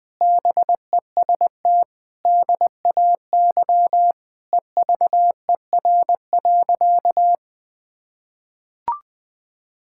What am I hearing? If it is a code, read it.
Morse code, heard as a keyed tone, about 20 wpm, 700 Hz. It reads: BEST DAY EVER.
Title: Sentences with 5-Word Phrases - 20wpm